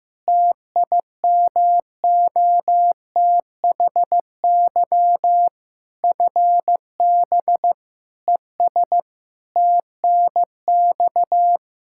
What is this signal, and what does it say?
Morse code, heard as a keyed tone, about 16 wpm, 700 Hz. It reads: TIMOTHY FB ES TNX
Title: Practice QSO - Exchange 2 - Info - 15wpm